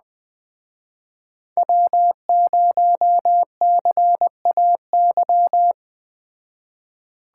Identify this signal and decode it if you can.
Morse code, heard as a keyed tone, about 20 wpm, 700 Hz. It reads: W0CAY